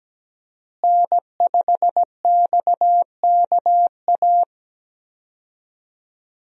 Morse code, heard as a keyed tone, about 17 wpm, 700 Hz. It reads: N5XKA
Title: USA Call Signs - 5 Characters - Repeated 2x - 17wpm